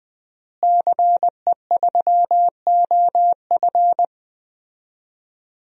Morse code, heard as a keyed tone, about 20 wpm, 700 Hz. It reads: CE3OF
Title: DXCC-100 Call Signs - 5 Characters - Repeated 2x - 20wpm